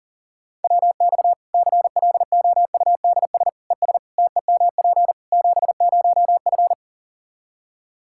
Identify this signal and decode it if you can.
Morse code, heard as a keyed tone, about 40 wpm, 700 Hz. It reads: WX CLOUDS ES TEMP 70F
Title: Practice QSO - Exchange 2 - Info - 40wpm